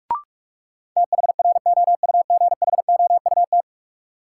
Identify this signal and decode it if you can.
Morse code, heard as a keyed tone, about 47 wpm, 700 Hz. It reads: THROUGHOUT
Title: Sentences from Top 800 Words - New Word Drill - 45wpm